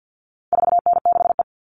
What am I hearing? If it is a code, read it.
Morse code, heard as a keyed tone, about 50 wpm, 700 Hz. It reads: 4N6E
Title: All Cut Numbers - 4-Digit 50wpm